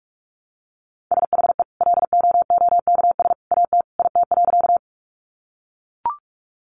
Morse code, heard as a keyed tone, about 44 wpm, 700 Hz. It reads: SHE LOOKS AT IT.
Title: Additional Easy Phrases 45wpm